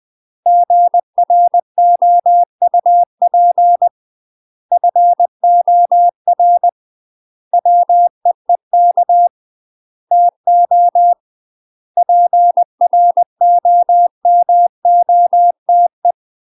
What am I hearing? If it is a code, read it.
Morse code, heard as a keyed tone, about 20 wpm, 700 Hz. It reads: GROUP FOR WEEK TO PROMOTE
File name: UK MASTER QSOs TRAINING / 20WPM / UK MASTER QSOs 0020